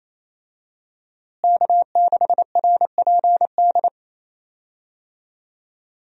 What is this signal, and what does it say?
Morse code, heard as a keyed tone, about 28 wpm, 700 Hz. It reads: K6RPD
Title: USA Call Signs - 5 Characters - Repeated 2x - 28wpm